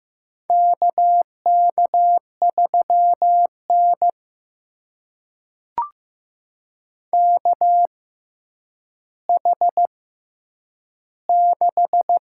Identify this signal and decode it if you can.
Morse code, heard as a keyed tone, about 15 wpm, spaced as for 9 wpm, 700 Hz. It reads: KK3N KH6
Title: USA Call Signs - 4 Characters - 6x Character Spacing - 15wpm